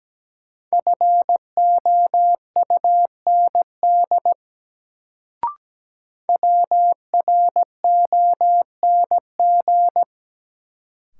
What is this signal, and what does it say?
Morse code, heard as a keyed tone, about 17 wpm, 700 Hz. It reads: FOUND WRONG
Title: Sets of 5 Words Encore - New Word Drill - 17wpm